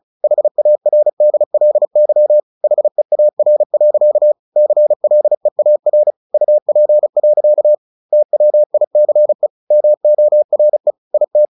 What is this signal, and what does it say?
Morse code, heard as a keyed tone, about 34 wpm, 600 Hz. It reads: HARDLY HEAR. CLEAR UP. TWICE MORE IT
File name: the old man and the sea 2 35wpm